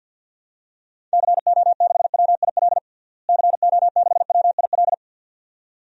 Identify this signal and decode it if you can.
Morse code, heard as a keyed tone, about 52 wpm, 700 Hz. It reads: CO6WIL CO6WIL
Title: DXCC-100 Call Signs - 6 Characters - Repeated 2x - 50wpm